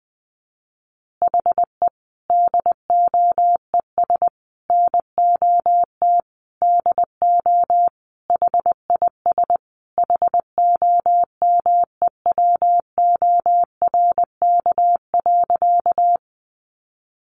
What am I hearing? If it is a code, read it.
Morse code, heard as a keyed tone, about 20 wpm, 700 Hz. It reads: HE DOES NOT DO HIS HOMEWORK.
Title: Additional Easy Phrases 20wpm